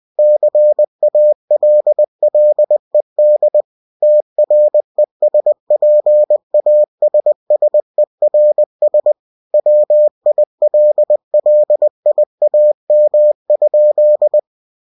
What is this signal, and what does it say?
Morse code, heard as a keyed tone, about 20 wpm, 600 Hz. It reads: CALLED TRESPASSERS WILLIAM?
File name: cw-books / winnie-the-pooh / winnie-20wpm-page-0041